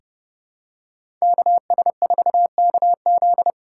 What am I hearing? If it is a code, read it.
Morse code, heard as a keyed tone, about 30 wpm, 690 Hz. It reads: KS4KZ
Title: USA Call Signs - 5 Characters - 6x Character Spacing - 30wpm